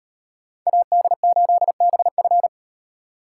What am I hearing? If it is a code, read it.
Morse code, heard as a keyed tone, about 36 wpm, 700 Hz. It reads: AD8BF